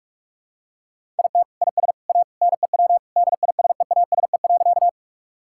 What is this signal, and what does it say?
Morse code, heard as a keyed tone, about 46 wpm, 700 Hz. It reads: IT IS A NEW DISEASE.
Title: Sentences from Top 1000 Words - Review Part 6 - 45wpm